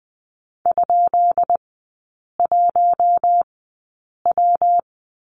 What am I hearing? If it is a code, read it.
Morse code, heard as a keyed tone, about 20 wpm, spaced as for 8 wpm, 700 Hz. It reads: ?1W
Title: 133 Lesson 22e - Question Mark - Three Characters 20wpm